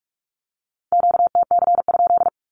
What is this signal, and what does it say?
Morse code, heard as a keyed tone, about 46 wpm, 700 Hz. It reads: QTC?
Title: Q-Codes with Spoken Meaning - 45wpm